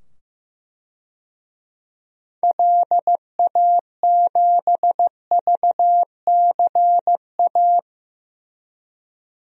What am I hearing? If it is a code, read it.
Morse code, heard as a keyed tone, about 15 wpm, 700 Hz. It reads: LA7VCA
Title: DXCC-100 Call Signs - 6 Characters - 15wpm